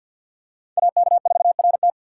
Morse code, heard as a keyed tone, about 46 wpm, 700 Hz. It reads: AK4RT
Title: USA Call Signs - 5 Characters - 2x Character Spacing - 50wpm